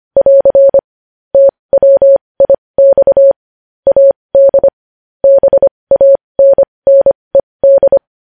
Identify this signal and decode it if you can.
Morse code, heard as a keyed tone, about 25 wpm, spaced as for 20 wpm, 550 Hz. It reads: <AR> TWIX AD BANNED
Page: news-headlines-in-morse-code-at-20-wpm